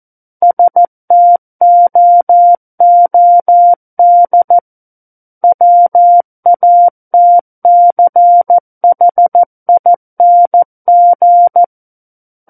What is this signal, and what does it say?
Morse code, heard as a keyed tone, about 14 wpm, 700 Hz. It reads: STOOD WATCHING
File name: Book One / 05-Chapter 5 - 14 WPM